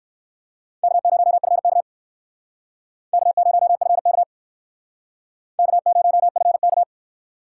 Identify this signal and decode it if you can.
Morse code, heard as a keyed tone, about 58 wpm, 700 Hz. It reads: K0FK K0FK K0FK